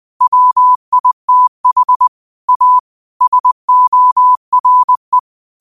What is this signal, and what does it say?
Morse code, heard as a keyed tone, about 20 wpm, 1000 Hz. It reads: WITH A SORE